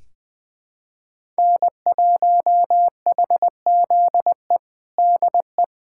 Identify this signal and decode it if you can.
Morse code, heard as a keyed tone, about 21 wpm, 700 Hz. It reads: N1HZE DE
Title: USA Callsign DE Callsign 20wpm